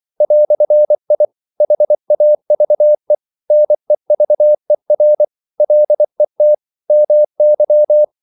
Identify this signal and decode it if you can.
Morse code, heard as a keyed tone, about 24 wpm, 600 Hz. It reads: "I HAVE NEVER LET MY
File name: CW / Practice / EnglishQuotations0424